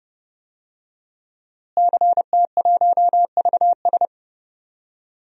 Morse code, heard as a keyed tone, about 32 wpm, 700 Hz. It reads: CT1VS
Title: DXCC-100 Call Signs - 5 Characters - Repeated 2x - 30wpm